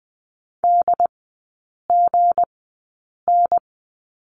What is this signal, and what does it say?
Morse code, heard as a keyed tone, about 20 wpm, spaced as for 8 wpm, 700 Hz. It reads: DGN
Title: 159 Lesson 26e - Letter g - Three Characters 20wpm